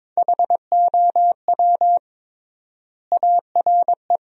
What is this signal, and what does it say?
Morse code, heard as a keyed tone, about 22 wpm, 700 Hz. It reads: HOW ARE
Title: 3x Word-Spacing - Sentences from Top 100 Words - 22wpm